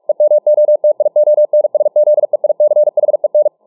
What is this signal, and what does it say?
Morse code, heard as a keyed tone, about 48 wpm, 600 Hz. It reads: EMOTIONSZEICHEN